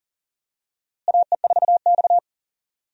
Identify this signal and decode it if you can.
Morse code, heard as a keyed tone, about 40 wpm, 700 Hz. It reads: AE4X
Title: USA Call Signs - 4 Characters - Repeated 3x - 40wpm